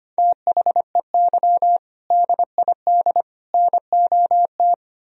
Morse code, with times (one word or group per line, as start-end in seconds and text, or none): 0.2-1.8 THEY
2.1-3.2 DID
3.5-4.7 NOT